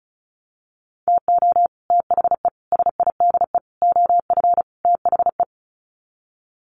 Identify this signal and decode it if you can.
Morse code, heard as a keyed tone, about 34 wpm, 700 Hz. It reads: TO THE SIDE OF THE
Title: Sets of 5 Words 35wpm